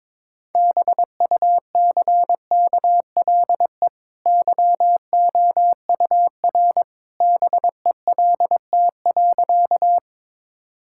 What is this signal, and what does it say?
Morse code, heard as a keyed tone, about 21 wpm, 690 Hz. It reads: BUCKLE YOUR BELT.
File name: practice / 22wpm / Easy Sentences with Hints - 22wpm